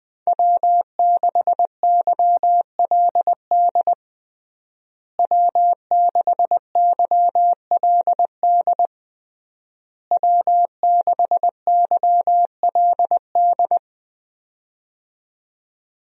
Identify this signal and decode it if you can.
Morse code, heard as a keyed tone, about 21 wpm, 700 Hz. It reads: W6YLD W6YLD W6YLD
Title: USA Call Signs - 5 Characters - Repeated 3x - 20wpm